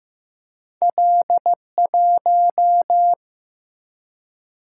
Morse code, heard as a keyed tone, about 15 wpm, 700 Hz. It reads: L1
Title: Single-Letter Single-Number 15wpm